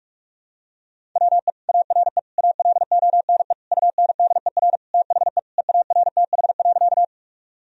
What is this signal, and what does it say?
Morse code, heard as a keyed tone, about 46 wpm, 700 Hz. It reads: WE ARE ALONE UNDER THE EARTH.
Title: Sentences from Top 1000 Words - Review Part 1 - 45wpm